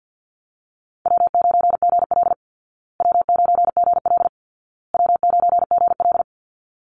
Morse code, heard as a keyed tone, about 48 wpm, 700 Hz. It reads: W9GL W9GL W9GL
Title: Easy Call Signs Repeated 3x - 50wpm